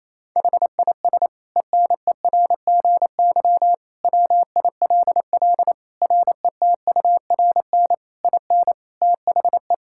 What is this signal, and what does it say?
Morse code, heard as a keyed tone, about 28 wpm, 700 Hz. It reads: HIS ENERGY WILL RETURN IN THE
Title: Sentences from Top 1000 Words - Review Part 6 - 28wpm